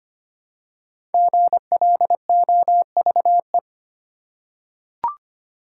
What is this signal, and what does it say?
Morse code, heard as a keyed tone, about 25 wpm, 700 Hz. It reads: GLOVE
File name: 5 Letter Words - First 3 Letters Given - 25wpm